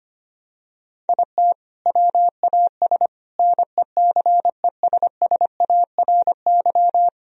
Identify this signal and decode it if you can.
Morse code, heard as a keyed tone, about 25 wpm, 700 Hz. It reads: IT WAS NECESSARY
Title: Sentences from Top 1000 Words - Review Part 1 - 25wpm